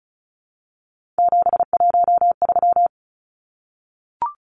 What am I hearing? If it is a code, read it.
Morse code, heard as a keyed tone, about 34 wpm, 700 Hz. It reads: 713
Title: Numbers - 3-Digit 35wpm